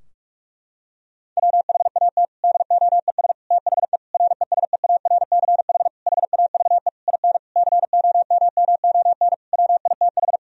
Morse code, heard as a keyed tone, about 46 wpm, 700 Hz. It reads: WHAT DOES THE RESEARCH HAVE IN COMMON WITH
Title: Sentences from Top 1000 Words - Review Part 4 - 45wpm